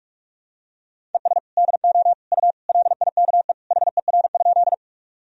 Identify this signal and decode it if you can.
Morse code, heard as a keyed tone, about 45 wpm, 700 Hz. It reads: ES DO U LIKE HER?
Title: Sentences from Top 100 Words and Abbreviations 45wpm